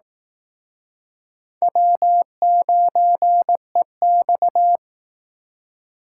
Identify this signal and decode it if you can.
Morse code, heard as a keyed tone, about 18 wpm, 700 Hz. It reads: W9EX
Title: Speed Racing - USA Call Signs 15wpm